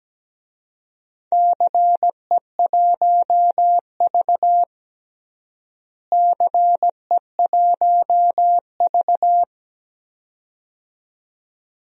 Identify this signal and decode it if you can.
Morse code, heard as a keyed tone, about 17 wpm, 700 Hz. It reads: CE1V CE1V
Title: DXCC-100 Call Signs - 4 Characters - Repeated 2x - 17wpm